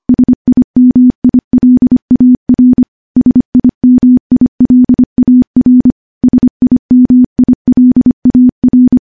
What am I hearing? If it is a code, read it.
Morse code, heard as a keyed tone, about 25 wpm, 250 Hz. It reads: SIMILAR SIMILAR SIMILAR
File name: morse / morse-training-25WPM